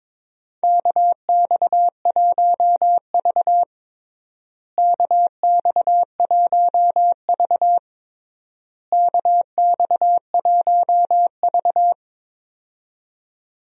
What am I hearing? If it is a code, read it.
Morse code, heard as a keyed tone, about 22 wpm, 700 Hz. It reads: KX1V KX1V KX1V
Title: USA Call Signs - 4 Characters - Repeated 3x - 22wpm